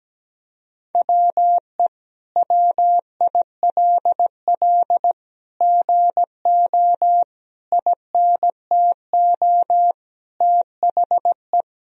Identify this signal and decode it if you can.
Morse code, heard as a keyed tone, about 17 wpm, 700 Hz. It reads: WE WILL GO INTO THE
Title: Sentences from Top 1000 Words - Review Part 4 - 17wpm